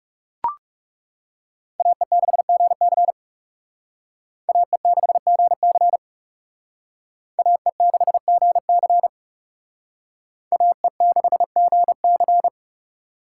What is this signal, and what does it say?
Morse code, heard as a keyed tone, about 38 wpm, 700 Hz. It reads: AE6GC AE6GC AE6GC AE6GC